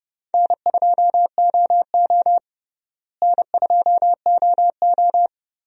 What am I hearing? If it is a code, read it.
Morse code, heard as a keyed tone, about 30 wpm, 700 Hz. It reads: N2OO N2OO
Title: USA Call Signs - 4 Characters - Repeated 3x - 30wpm